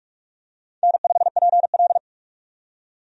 Morse code, heard as a keyed tone, about 45 wpm, 700 Hz. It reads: N5PL
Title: USA Call Signs - 4 Characters - 45wpm